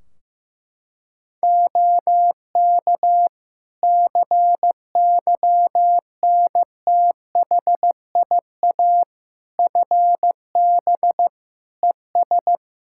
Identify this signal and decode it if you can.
Morse code, heard as a keyed tone, about 15 wpm, 700 Hz. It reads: OK CYNTHIA FB ES
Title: K1USN SST QSO Practice 15wpm